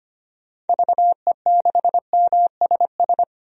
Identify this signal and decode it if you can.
Morse code, heard as a keyed tone, about 25 wpm, 700 Hz. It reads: VE6MSS
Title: DXCC-100 Call Signs - 6 Characters - 25wpm